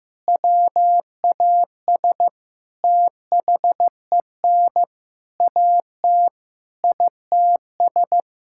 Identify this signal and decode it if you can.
Morse code, heard as a keyed tone, about 15 wpm, 700 Hz. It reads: WAS THEN AT ITS
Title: Sentences from Top 1600 Words 15wpm